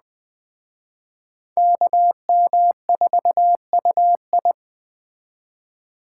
Morse code, heard as a keyed tone, about 19 wpm, 700 Hz. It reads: KM4UI